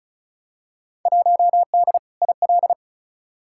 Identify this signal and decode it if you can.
Morse code, heard as a keyed tone, about 34 wpm, 700 Hz. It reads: 1D IL
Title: Field Day Exchanges - 35wpm